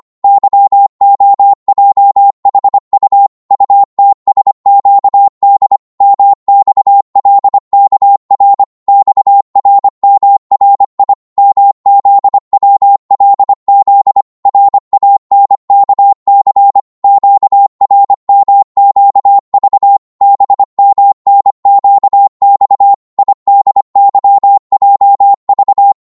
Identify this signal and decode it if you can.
Morse code, heard as a keyed tone, about 25 wpm, 800 Hz. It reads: YOJHU UTSQD MXLKR XRMRI MZWLZ RANKC QRMQV BMNQX IDYJV
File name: Morse / ltr-25wpm